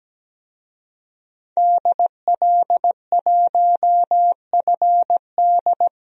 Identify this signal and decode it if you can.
Morse code, heard as a keyed tone, about 17 wpm, 700 Hz. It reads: DL1FD
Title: DXCC-100 Call Signs - 5 Characters - 17wpm